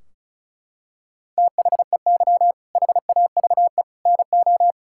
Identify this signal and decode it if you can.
Morse code, heard as a keyed tone, about 36 wpm, 700 Hz. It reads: THEY HAVE NO
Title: Sentences from Top 1600 Words 35wpm